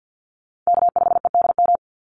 Morse code, heard as a keyed tone, about 50 wpm, 700 Hz. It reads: K5EDM
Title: Call Signs Repeated 3x - 50wpm